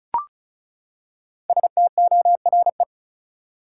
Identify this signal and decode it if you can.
Morse code, heard as a keyed tone, about 36 wpm, 700 Hz. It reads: STORE
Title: Five Letter Words - 35wpm